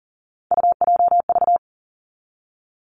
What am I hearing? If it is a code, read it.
Morse code, heard as a keyed tone, about 38 wpm, 700 Hz. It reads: UJV
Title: Three Challenging Letter-Number Combinations 40wpm